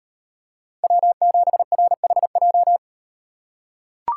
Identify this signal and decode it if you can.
Morse code, heard as a keyed tone, about 38 wpm, 700 Hz. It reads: W7RHJ